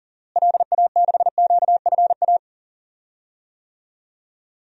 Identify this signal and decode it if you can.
Morse code, heard as a keyed tone, about 38 wpm, 700 Hz. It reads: LA6QFA
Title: DXCC-100 Call Signs - 6 Characters - Repeated 2x - 40wpm